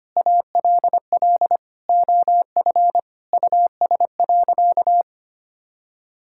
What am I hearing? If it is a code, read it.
Morse code, heard as a keyed tone, about 25 wpm, 700 Hz. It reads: ALL OF US.
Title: Sentences from Top 1000 Words - Review Part 6 - 25wpm